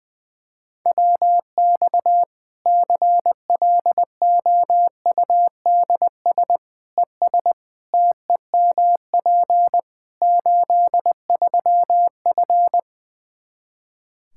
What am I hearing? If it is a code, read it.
Morse code, heard as a keyed tone, about 20 wpm, 700 Hz. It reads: WX CLOUDS ES TEMP 83F
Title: K1USN SST QSO Practice 20wpm